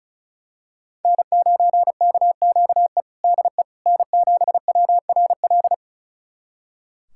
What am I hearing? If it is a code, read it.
Morse code, heard as a keyed tone, about 34 wpm, 690 Hz. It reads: N9KQE DE N7WRL
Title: USA Callsign DE Callsign - 5 Characters - 35wpm